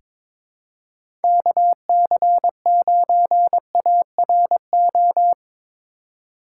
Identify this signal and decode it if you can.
Morse code, heard as a keyed tone, about 22 wpm, 700 Hz. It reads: KC9ARO
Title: USA Call Signs - 6 Characters - 4x Character Spacing - 22wpm